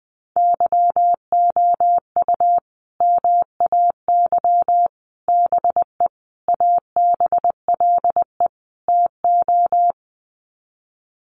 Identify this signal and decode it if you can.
Morse code, heard as a keyed tone, about 20 wpm, 700 Hz. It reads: YOU MAY BE ABLE TO
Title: Sets of 5 Words Encore 20wpm